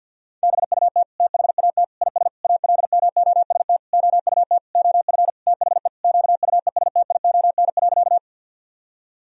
Morse code, heard as a keyed tone, about 49 wpm, 700 Hz. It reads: BUT THAT IS ALMOST OUT OF THE QUESTION.
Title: Sentences from Top 1000 Words - Review Part 4 - 50wpm